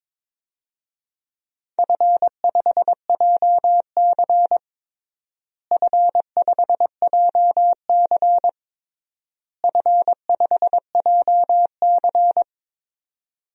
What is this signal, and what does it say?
Morse code, heard as a keyed tone, about 22 wpm, 700 Hz. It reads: F5JC F5JC F5JC
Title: DXCC-100 Call Signs - 4 Characters - Repeated 3x - 22wpm